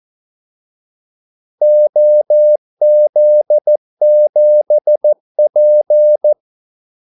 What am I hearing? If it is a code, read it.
Morse code, heard as a keyed tone, about 14 wpm, 600 Hz. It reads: OZ7P
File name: Callsigns-14wpm